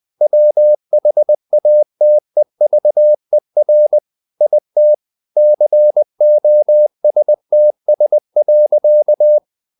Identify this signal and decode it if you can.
Morse code, heard as a keyed tone, about 20 wpm, 600 Hz. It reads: WHATEVER IT COSTS.
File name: the old man and the sea / the old man and the sea part 1 20wpm